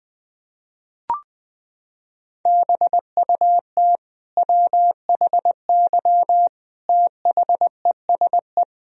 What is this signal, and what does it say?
Morse code, heard as a keyed tone, about 20 wpm, 700 Hz. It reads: BUT WHY THESE
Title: Sentences from Top 1000 Words - Review Part 1 - 20wpm